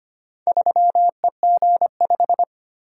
Morse code, heard as a keyed tone, about 25 wpm, 700 Hz. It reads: 3EG5
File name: practice / 25wpm / Four Letter-Number Combinations - 25wpm